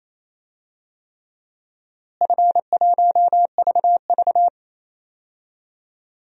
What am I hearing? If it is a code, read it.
Morse code, heard as a keyed tone, about 27 wpm, 700 Hz. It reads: F1VV
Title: DXCC-100 Call Signs - 4 Characters - Repeated 3x - 28wpm